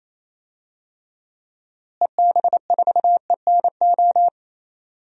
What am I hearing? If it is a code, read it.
Morse code, heard as a keyed tone, about 28 wpm, 700 Hz. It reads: EB4ENO
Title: DXCC-100 Call Signs - 6 Characters - Repeated 2x - 28wpm